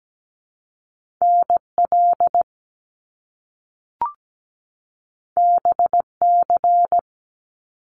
Morse code, heard as a keyed tone, about 17 wpm, 700 Hz. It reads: NL BC
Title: Canadian Provinces and Territories 17wpm